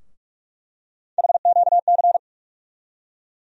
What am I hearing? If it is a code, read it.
Morse code, heard as a keyed tone, about 41 wpm, 700 Hz. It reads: HQC
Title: USA Call Sign Postfixes - 3 Letters 45wpm